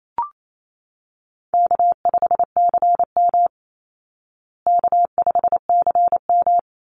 Morse code, heard as a keyed tone, about 28 wpm, 700 Hz. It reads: K5CM K5CM
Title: Easy Call Signs Repeated 2x - 28wpm